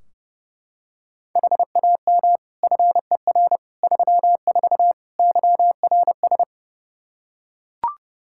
Morse code, heard as a keyed tone, about 29 wpm, 700 Hz. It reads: HAM FER 34 YRS
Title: Practice QSO - Exchange 3 - More Info - 30wpm